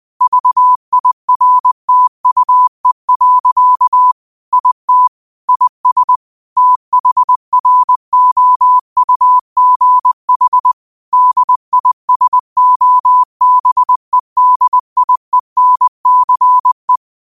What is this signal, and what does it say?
Morse code, heard as a keyed tone, about 20 wpm, 1000 Hz. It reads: VIRTUE. IT IS THROUGH DISOBEDIENCE